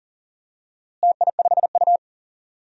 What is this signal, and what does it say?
Morse code, heard as a keyed tone, about 40 wpm, 700 Hz. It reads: TI5U
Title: DXCC-100 Call Signs - 4 Characters - Repeated 3x - 40wpm